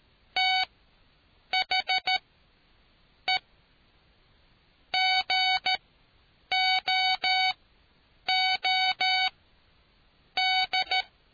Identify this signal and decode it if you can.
Morse code, heard as a keyed tone, about 13 wpm, spaced as for 7 wpm, 750 Hz. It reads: THE GOOD